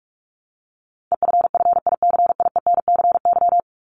Morse code, heard as a keyed tone, about 44 wpm, 700 Hz. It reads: EFFICIENCY
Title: Top 1501-1750 Words 45wpm